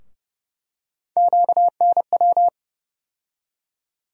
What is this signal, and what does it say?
Morse code, heard as a keyed tone, about 30 wpm, 700 Hz. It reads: QNW
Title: USA Call Sign Postfixes - 3 Letters 30wpm